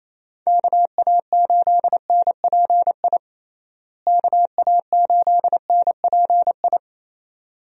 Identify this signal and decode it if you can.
Morse code, heard as a keyed tone, about 27 wpm, 700 Hz. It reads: KA8NPI KA8NPI
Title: USA Call Signs - 6 Characters - Repeated 2x - 28wpm